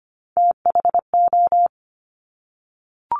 Morse code, heard as a keyed tone, about 25 wpm, 700 Hz. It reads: THO
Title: Top Three Letter Combinations in Words 25wpm